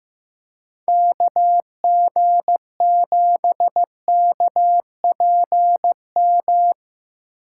Morse code, heard as a keyed tone, about 15 wpm, 700 Hz. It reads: KG7KPM
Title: USA Call Signs - 6 Characters - 2x Character Spacing - 15wpm